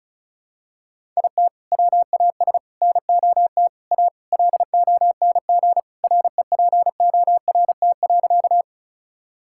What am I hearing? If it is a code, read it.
Morse code, heard as a keyed tone, about 36 wpm, 700 Hz. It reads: IT WAS NOT A LONG REPORT.
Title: Sentences from Top 1000 Words - Review Part 6 - 35wpm